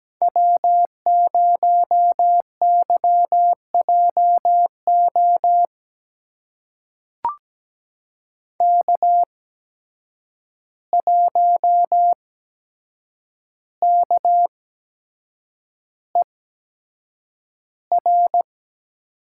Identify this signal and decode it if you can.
Morse code, heard as a keyed tone, about 17 wpm, spaced as for 7 wpm, 700 Hz. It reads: W0YJO K1KER
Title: USA Call Signs - 5 Characters - 8x Character Spacing - 17wpm